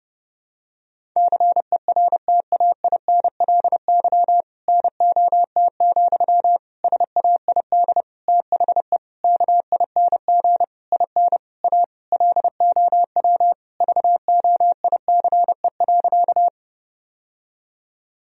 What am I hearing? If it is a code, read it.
Morse code, heard as a keyed tone, about 30 wpm, 700 Hz. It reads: CERTAINLY NOT, SAID THE KING IN A LOW VOICE.
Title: Sentences from Top 1000 Words - Review Part 1 - 30wpm